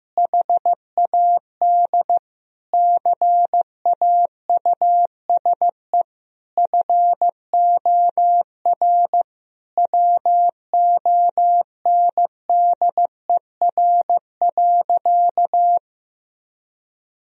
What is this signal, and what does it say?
Morse code, heard as a keyed tone, about 15 wpm, 700 Hz. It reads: HAD CAUSE FOR WONDER.
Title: Sentences from Top 1000 Words - Review Part 3 - 15wpm